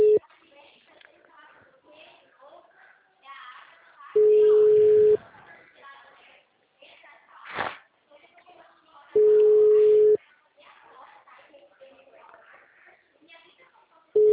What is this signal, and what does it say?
Ringback tone (Europe, 425 Hz), with speech.
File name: out- -2011- - - .55